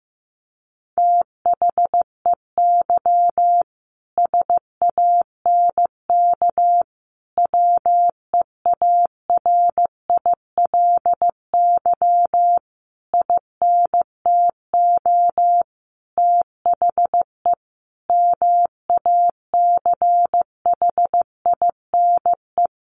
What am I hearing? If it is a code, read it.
Morse code, heard as a keyed tone, about 15 wpm, 700 Hz. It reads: THEY SANK WEARILY INTO THE MACHINE
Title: A Sound Of Thunder 15wpm